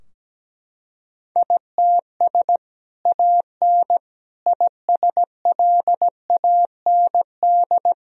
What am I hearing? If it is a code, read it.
Morse code, heard as a keyed tone, about 17 wpm, 700 Hz. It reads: ITS AN ISLAND
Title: Sentences from Top 1000 Words - Review Part 3 - 17wpm